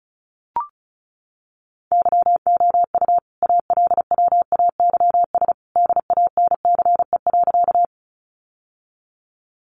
Morse code, heard as a keyed tone, about 34 wpm, 700 Hz. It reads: YOU ALWAYS DANCE.
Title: Sentences from Top 700 Words 35wpm